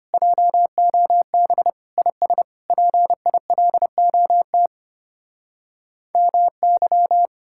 Morse code, heard as a keyed tone, about 32 wpm, 700 Hz. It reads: JOB IS PILOT MY